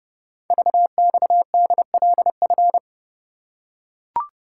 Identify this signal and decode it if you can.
Morse code, heard as a keyed tone, about 29 wpm, 700 Hz. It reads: VXDLF
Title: Code Groups - 5 Letters - 30wpm